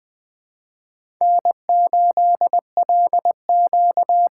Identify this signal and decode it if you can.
Morse code, heard as a keyed tone, about 20 wpm, 700 Hz. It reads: N8LQ
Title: USA Call Signs - 5 Characters - 6x Character Spacing - 20wpm